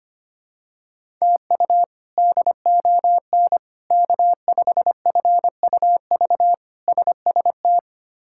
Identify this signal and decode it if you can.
Morse code, heard as a keyed tone, about 25 wpm, 700 Hz. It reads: TU DON K5FUV SST
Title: K1USN SST QSO Practice - 25wpm